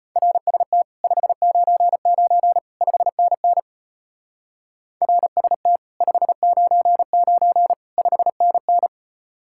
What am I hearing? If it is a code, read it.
Morse code, heard as a keyed tone, about 35 wpm, 700 Hz. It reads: RST 599 5NN RST 599 5NN